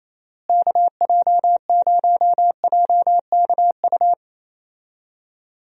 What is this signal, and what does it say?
Morse code, heard as a keyed tone, about 28 wpm, 700 Hz. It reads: KJ0JKU
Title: USA Call Signs - 6 Characters - Repeated 2x - 28wpm